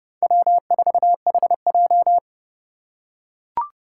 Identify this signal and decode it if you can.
Morse code, heard as a keyed tone, about 28 wpm, 700 Hz. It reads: W4HJ